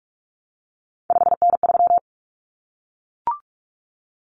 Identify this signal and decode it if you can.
Morse code, heard as a keyed tone, about 44 wpm, 700 Hz. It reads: 5N3
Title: Common Cut Numbers - 3-Digit 45wpm